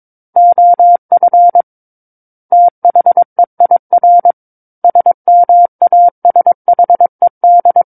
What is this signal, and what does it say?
Morse code, heard as a keyed tone, about 22 wpm, 700 Hz. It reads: OF THEIR SMASHED
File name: Book One / 13-Chapter 13 - 22 WPM